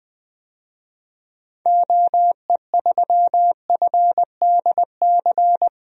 Morse code, heard as a keyed tone, about 20 wpm, 700 Hz. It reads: OE3FDC
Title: DXCC-100 Call Signs - 6 Characters - Repeated 2x - 20wpm